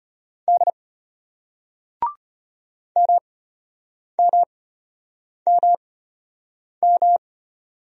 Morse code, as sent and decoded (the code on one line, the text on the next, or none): -.. / -- -- -- --
D MMMM